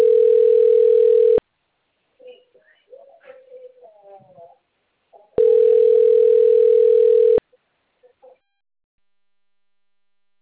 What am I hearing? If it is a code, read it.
Ringback tone (North America), 440 + 480 Hz.